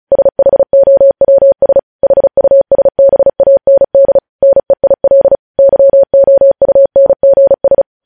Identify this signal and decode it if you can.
Morse code, heard as a keyed tone, about 36 wpm, 550 Hz. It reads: SHOWS HUSBAND NEIL YOUNGS
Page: news headlines in morse code at 35 wpm